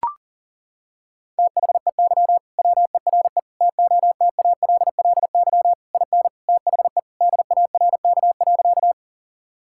Morse code, heard as a keyed tone, about 40 wpm, 700 Hz. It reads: THEY WERE TOTALLY IN THE DARK.
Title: Sentences from Top 1600 Words 40wpm